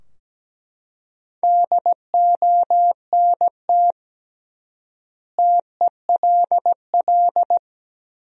Morse code, heard as a keyed tone, about 17 wpm, 700 Hz. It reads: DONT TELL
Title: 3x Word-Spacing - Sentences from Top 300 Words - 17wpm